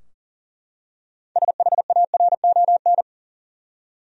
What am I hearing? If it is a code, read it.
Morse code, heard as a keyed tone, about 40 wpm, 700 Hz. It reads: SHARON
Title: USA Common Names - 40wpm